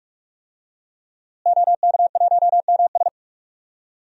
Morse code, heard as a keyed tone, about 41 wpm, 700 Hz. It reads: OK1MS
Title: DXCC-100 Call Signs - 5 Characters - Repeated 2x - 45wpm